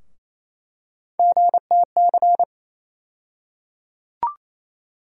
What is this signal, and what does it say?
Morse code, heard as a keyed tone, about 28 wpm, 700 Hz. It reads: GTC
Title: USA Call Sign Postfixes - 3 Letters 28wpm